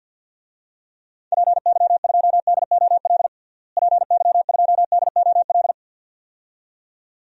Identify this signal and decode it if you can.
Morse code, heard as a keyed tone, about 52 wpm, 700 Hz. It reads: PY2DOL PY2DOL
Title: DXCC-100 Call Signs - 6 Characters - Repeated 2x - 50wpm